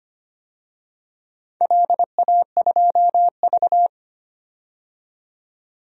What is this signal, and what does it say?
Morse code, heard as a keyed tone, about 25 wpm, 700 Hz. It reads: LA2V
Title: DXCC-100 Call Signs - 4 Characters - 25wpm